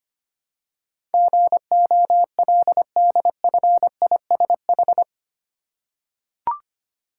Morse code, heard as a keyed tone, about 25 wpm, 700 Hz. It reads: GOLDFISH
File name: practice / 25wpm / Easy Words with Hints - 25wpm